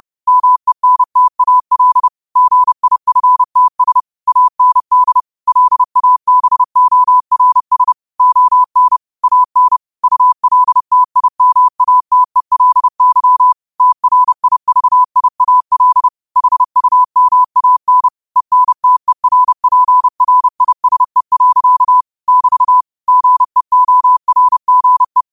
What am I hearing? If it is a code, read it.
Morse code, heard as a keyed tone, about 30 wpm, 1000 Hz. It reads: MENTAL GIFTS AND LABORS ON AN ULTIMATELY TRIVIAL HUMAN ENTERPRISE. <BT> GEORGE
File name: qotdfiles / 30wpm